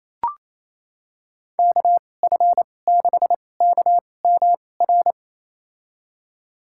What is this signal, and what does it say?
Morse code, heard as a keyed tone, about 28 wpm, spaced as for 20 wpm, 700 Hz. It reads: KF6KMR
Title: USA Call Signs - 2x Character Spacing - 28wpm